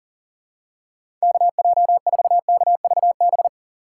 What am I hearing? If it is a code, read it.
Morse code, heard as a keyed tone, about 40 wpm, 700 Hz. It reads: KJ4KVB
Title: USA Call Signs - 6 Characters - 2x Character Spacing - 40wpm